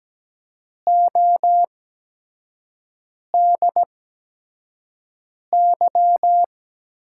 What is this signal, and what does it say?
Morse code, heard as a keyed tone, about 17 wpm, spaced as for 5 wpm, 700 Hz. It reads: ODY
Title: USA Call Signs - 5 Characters - 8x Character Spacing - 17wpm